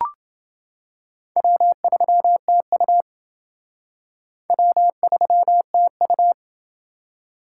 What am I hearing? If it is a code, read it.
Morse code, heard as a keyed tone, about 28 wpm, 700 Hz. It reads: W3TU W3TU